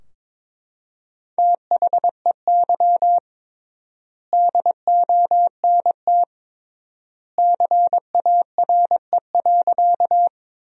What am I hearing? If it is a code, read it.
Morse code, heard as a keyed tone, about 22 wpm, 700 Hz. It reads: THEY DONT CARE.
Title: 3x Word-Spacing - Sentences from Top 300 Words - 22wpm